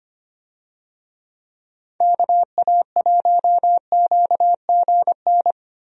Morse code, heard as a keyed tone, about 25 wpm, 700 Hz. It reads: KA1QGN
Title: USA Call Signs - 6 Characters - Repeated 3x - 25wpm